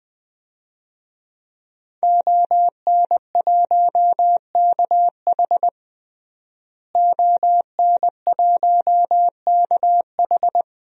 Morse code, heard as a keyed tone, about 20 wpm, 700 Hz. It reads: ON1KH ON1KH
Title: DXCC-100 Call Signs - 5 Characters - Repeated 3x - 20wpm